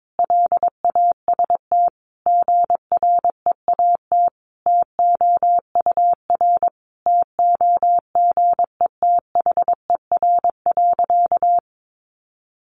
Morse code, heard as a keyed tone, about 22 wpm, 700 Hz. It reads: LAST GREAT TOUR TOGETHER.
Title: Sentences from Top 1400 Words 22wpm